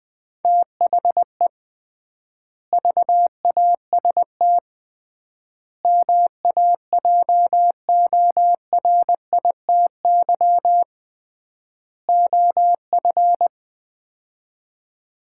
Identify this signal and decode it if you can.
Morse code, heard as a keyed tone, about 21 wpm, 700 Hz. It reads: THE VAST MAJORITY OF
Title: 3x Word-Spacing - Sets of 4 Words - 20wpm